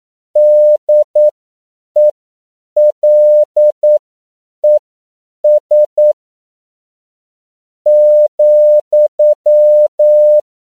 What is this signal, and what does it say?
Morse code, heard as a keyed tone, about 9 wpm, spaced as for 7 wpm, 600 Hz. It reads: DELES ,
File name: CW / 10PPM